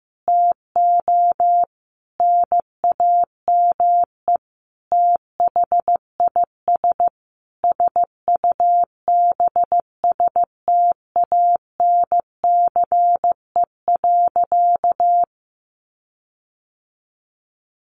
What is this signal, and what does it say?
Morse code, heard as a keyed tone, about 14 wpm, 700 Hz. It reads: TO NAME THIS SUBSTANCE.
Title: Sentences from Top 1500 Words 15wpm